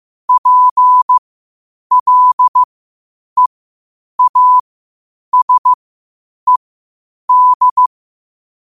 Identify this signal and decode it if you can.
Morse code, heard as a keyed tone, about 15 wpm, spaced as for 8 wpm, 1000 Hz. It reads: PLEASED
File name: qotdfiles / 8wpm